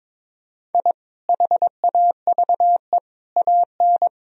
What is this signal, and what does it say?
Morse code, heard as a keyed tone, about 22 wpm, 700 Hz. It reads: I HAVE AN
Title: Sentences from Top 1000 Words - Review Part 2 - 22wpm